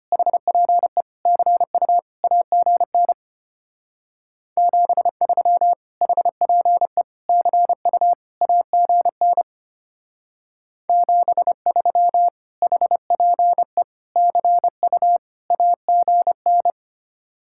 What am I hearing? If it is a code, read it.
Morse code, heard as a keyed tone, about 30 wpm, 700 Hz. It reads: HPE CU AGN 73 HPE CU AGN 73 HPE CU AGN
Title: Speed-Racing Common QSO Phrases Encore - 25wpm